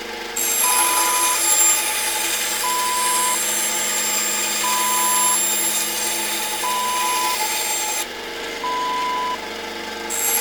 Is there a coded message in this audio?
No.